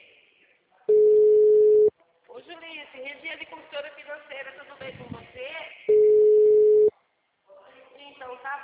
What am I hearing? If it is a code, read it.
Ringback tone (Europe, 425 Hz), with speech.
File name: out- -2006- - - .41